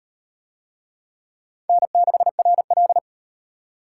Morse code, heard as a keyed tone, about 34 wpm, 700 Hz. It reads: N6RL